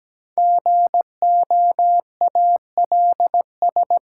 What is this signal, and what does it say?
Morse code, heard as a keyed tone, about 17 wpm, 700 Hz. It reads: GOALS
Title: Sentences from Top 1000 Words - Review Part 6 - New Word Drill - 17wpm